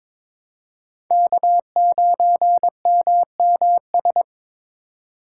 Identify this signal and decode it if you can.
Morse code, heard as a keyed tone, about 22 wpm, 700 Hz. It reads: K9MMS
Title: K1USN SST QSO Practice - 22wpm